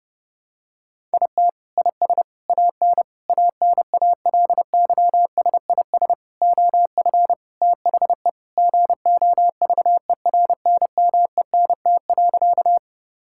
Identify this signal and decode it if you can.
Morse code, heard as a keyed tone, about 30 wpm, 700 Hz. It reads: IT IS AN ANALYSIS OF THE GOVERNMENT.
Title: Sentences from Top 1000 Words - Review Part 5 - 30wpm